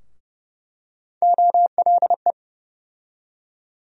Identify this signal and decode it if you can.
Morse code, heard as a keyed tone, about 31 wpm, 700 Hz. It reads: OLE
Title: USA Call Sign Postfixes - 3 Letters 30wpm